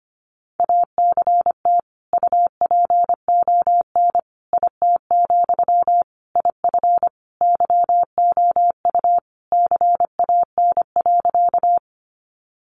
Morse code, heard as a keyed tone, about 25 wpm, 700 Hz. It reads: ACT UPON IT, IF YOU CAN.
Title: Sentences from Top 1100 Words 25wpm